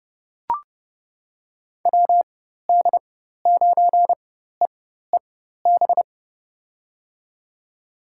Morse code, heard as a keyed tone, about 30 wpm, spaced as for 14 wpm, 700 Hz. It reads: WD9EEB
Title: USA Call Signs - 6 Characters - 4x Character Spacing - 30wpm